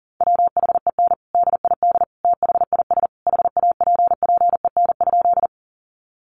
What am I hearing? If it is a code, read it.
Morse code, heard as a keyed tone, about 40 wpm, 700 Hz. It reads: WHEN DID THIS HAPPEN?
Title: Sentences from Top 400 Words 40wpm